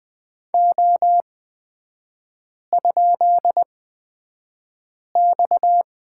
Morse code, heard as a keyed tone, about 20 wpm, spaced as for 5 wpm, 690 Hz. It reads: O?X
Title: 229 Lesson 37e - Letter x - Three Characters 20wpm